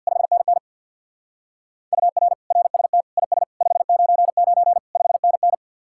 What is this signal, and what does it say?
Morse code, heard as a keyed tone, about 66 wpm, 700 Hz. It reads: 5NN UR RST IS 599 5NN